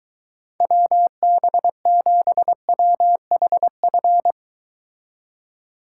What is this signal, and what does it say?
Morse code, heard as a keyed tone, about 23 wpm, 700 Hz. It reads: WB7WHF